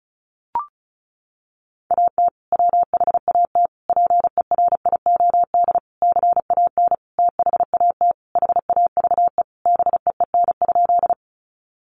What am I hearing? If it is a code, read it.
Morse code, heard as a keyed tone, about 34 wpm, 700 Hz. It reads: AT WHAT PERIOD CAN THAT HAVE BEEN?
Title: Sentences from Top 800 Words 35wpm